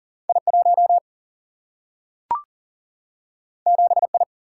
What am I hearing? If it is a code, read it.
Morse code, heard as a keyed tone, about 40 wpm, 700 Hz. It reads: I1 7I
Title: Single-Letter Single-Number 40wpm